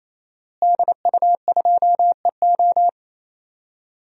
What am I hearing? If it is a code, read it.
Morse code, heard as a keyed tone, about 28 wpm, 700 Hz. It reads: DU2EO
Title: DXCC-100 Call Signs - 5 Characters - Repeated 3x - 28wpm